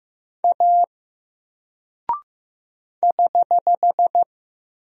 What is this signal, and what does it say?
Morse code, heard as a keyed tone, about 15 wpm, 700 Hz. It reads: A <HH>
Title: Single Character (Letter:Number:Punctuation:Prosign) 15wpm